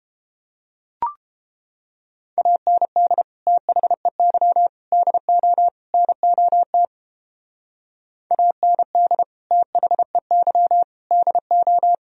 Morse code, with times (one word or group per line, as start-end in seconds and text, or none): 2.4-3.2 AND
3.5-4.7 THEY
4.9-5.7 DO
5.9-6.9 NOT
8.3-9.2 AND
9.5-10.8 THEY
11.1-11.9 DO